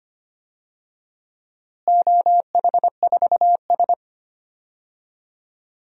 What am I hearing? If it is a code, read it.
Morse code, heard as a keyed tone, about 25 wpm, 700 Hz. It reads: OH4S
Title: DXCC-100 Call Signs - 4 Characters - Repeated 3x - 25wpm